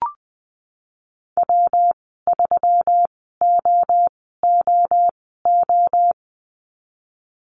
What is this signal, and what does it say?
Morse code, heard as a keyed tone, about 20 wpm, spaced as for 15 wpm, 700 Hz. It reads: W3OOO
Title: 122 Lesson 20g - Number 3 - Call Signs 20wpm